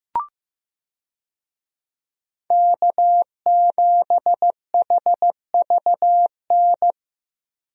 Morse code, heard as a keyed tone, about 15 wpm, 700 Hz. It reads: K7HVN
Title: USA Call Signs - 5 Characters - 15wpm